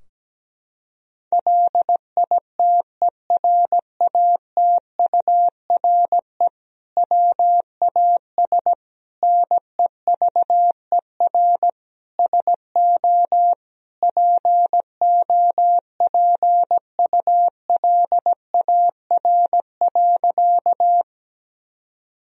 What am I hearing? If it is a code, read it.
Morse code, heard as a keyed tone, about 17 wpm, 700 Hz. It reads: LITERATURE WAS NEVER SO POPULAR.
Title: Sentences from Top 1000 Words - Review Part 3 - 17wpm